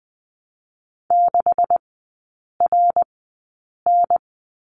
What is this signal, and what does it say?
Morse code, heard as a keyed tone, about 20 wpm, spaced as for 8 wpm, 700 Hz. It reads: 6RN
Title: 127 Lesson 21e - Number 6 - Three Characters 20wpm